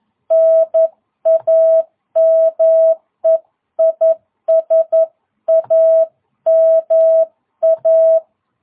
Morse code, heard as a keyed tone, about 11 wpm, 660 Hz. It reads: NAME IS AMA
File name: jh8cbh / 2022cw